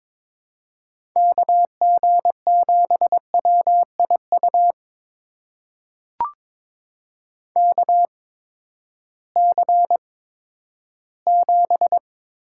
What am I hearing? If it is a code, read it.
Morse code, heard as a keyed tone, about 22 wpm, spaced as for 13 wpm, 700 Hz. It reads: KG7WIU KC7
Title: USA Call Signs - 6 Characters - 8x Character Spacing - 22wpm